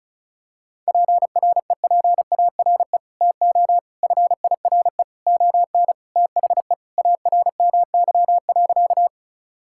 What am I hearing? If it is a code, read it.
Morse code, heard as a keyed tone, about 36 wpm, 700 Hz. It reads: PREPARE TO FIRE ON THE ARMY.
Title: Sentences from Top 1000 Words - Review Part 6 - 35wpm